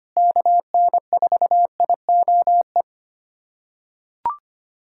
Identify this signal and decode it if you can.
Morse code, heard as a keyed tone, about 25 wpm, 700 Hz. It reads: KN4IOE
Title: USA Callsign DE Callsign 25wpm